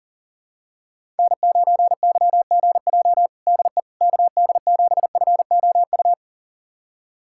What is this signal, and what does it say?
Morse code, heard as a keyed tone, about 37 wpm, 700 Hz. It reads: N9YGJ DE KD7FOU
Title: USA Callsign DE Callsign 40wpm